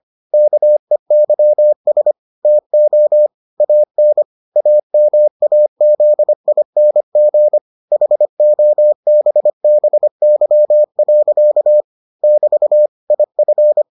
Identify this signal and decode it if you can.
Morse code, heard as a keyed tone, about 25 wpm, 600 Hz. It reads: KEYS TO AN AMAZING HOBBY. <BT> IF